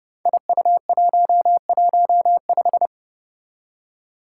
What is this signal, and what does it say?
Morse code, heard as a keyed tone, about 30 wpm, 700 Hz. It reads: IU115
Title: Five Challenging Letter-Number Combinations 30wpm